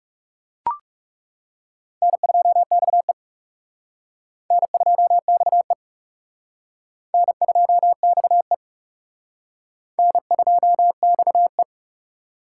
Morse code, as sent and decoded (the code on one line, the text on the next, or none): -. ..--- -..- . / -. ..--- -..- . / -. ..--- -..- . / -. ..--- -..- .
N2XE N2XE N2XE N2XE